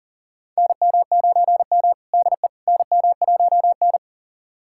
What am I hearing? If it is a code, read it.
Morse code, heard as a keyed tone, about 38 wpm, 700 Hz. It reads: NM9M DE NM1N
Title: USA Callsign DE Callsign - 4 Characters - 40wpm